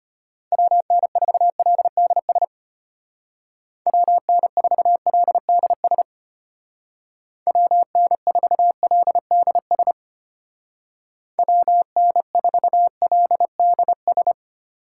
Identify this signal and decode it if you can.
Morse code, heard as a keyed tone, about 32 wpm, 700 Hz. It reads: WN4LDS WN4LDS WN4LDS WN4LDS